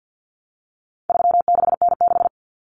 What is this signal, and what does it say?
Morse code, heard as a keyed tone, about 50 wpm, 700 Hz. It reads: 36N6
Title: Common Cut Numbers - 4-Digit 50wpm